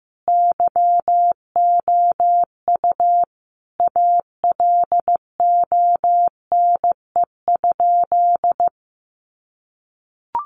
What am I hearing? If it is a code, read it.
Morse code, heard as a keyed tone, about 15 wpm, 700 Hz. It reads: YOU ALONE?
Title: Sentences from Top 1000 Words 15wpm